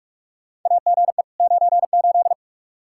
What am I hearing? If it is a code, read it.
Morse code, heard as a keyed tone, about 41 wpm, 700 Hz. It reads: AGE 98
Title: Practice QSO - Exchange 3 - More Info - 45wpm